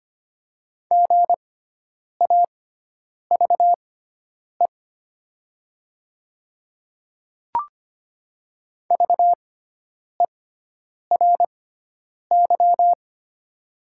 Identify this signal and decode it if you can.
Morse code, heard as a keyed tone, about 25 wpm, spaced as for 9 wpm, 700 Hz. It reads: GAVE VERY
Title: 6x Character Spacing - Top 500 Words - 25wpm